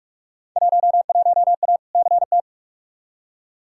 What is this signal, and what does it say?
Morse code, heard as a keyed tone, about 41 wpm, 700 Hz. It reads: 11A CT
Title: Field Day Exchanges Encore - 45wpm